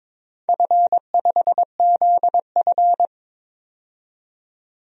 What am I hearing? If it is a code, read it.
Morse code, heard as a keyed tone, about 22 wpm, 700 Hz. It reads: F5ZF
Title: DXCC-100 Call Signs - 4 Characters - 22wpm